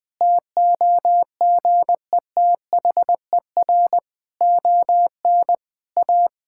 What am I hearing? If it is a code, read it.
Morse code, heard as a keyed tone, about 20 wpm, 700 Hz. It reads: TOGETHER ON A
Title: Sentences from Top 1000 Words - Review Part 5 - 20wpm